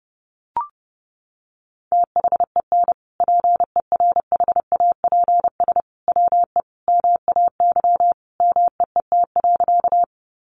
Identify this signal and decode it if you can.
Morse code, heard as a keyed tone, about 30 wpm, 700 Hz. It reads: THEN PERHAPS WE MAY MEET.
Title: Sentences from Top 1100 Words 30wpm